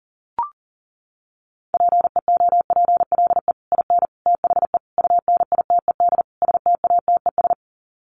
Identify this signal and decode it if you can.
Morse code, heard as a keyed tone, about 40 wpm, 700 Hz. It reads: PEOPLE IN THE UNITED STATES
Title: Sets of 5 Words Encore 40wpm